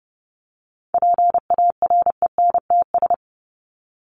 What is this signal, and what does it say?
Morse code, heard as a keyed tone, about 30 wpm, 700 Hz. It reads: PARENTS
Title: ES:S Suffix - Top 100 Words with Base Words 30wpm